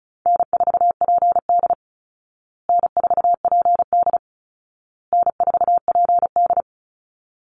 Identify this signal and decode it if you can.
Morse code, heard as a keyed tone, about 34 wpm, 700 Hz. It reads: N4PD N4PD N4PD
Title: Easy Call Signs Repeated 3x - 35wpm